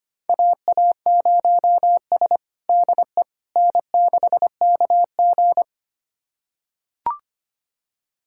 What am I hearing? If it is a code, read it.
Morse code, heard as a keyed tone, about 25 wpm, 700 Hz. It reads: AA0S DE N6KG
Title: USA Callsign DE Callsign - 4 Characters - 25wpm